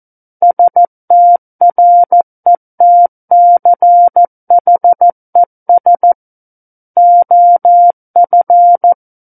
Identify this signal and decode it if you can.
Morse code, heard as a keyed tone, about 14 wpm, 700 Hz. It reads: STRETCHES OF
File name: Book One / 05-Chapter 5 - 14 WPM